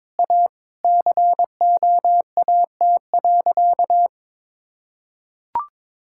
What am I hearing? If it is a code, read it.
Morse code, heard as a keyed tone, about 22 wpm, 700 Hz. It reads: A COAT.
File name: practice / 22wpm / Sentences with 4-Word Phrases - 22wpm